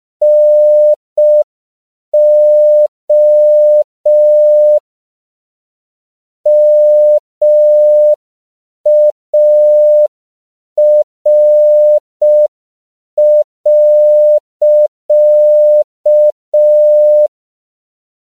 Morse code, heard as a keyed tone, about 5 wpm, 600 Hz. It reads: NO MAR.
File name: CW / 5PPM